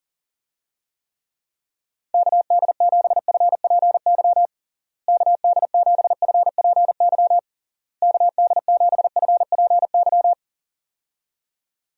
Morse code, heard as a keyed tone, about 40 wpm, 700 Hz. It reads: KD7FPY KD7FPY KD7FPY
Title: USA Call Signs - 6 Characters - Repeated 3x - 40wpm